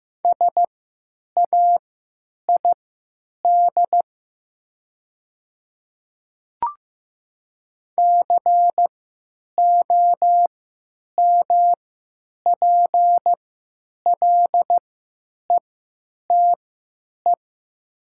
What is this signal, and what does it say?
Morse code, heard as a keyed tone, about 15 wpm, spaced as for 8 wpm, 700 Hz. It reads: SAID COMPLETE
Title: 3x Character Spacing - Top 500 Words - 15wpm